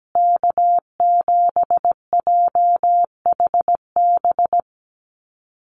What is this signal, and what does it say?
Morse code, heard as a keyed tone, about 17 wpm, 700 Hz. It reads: K7JHB
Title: USA Call Signs Repeated 2x 17wpm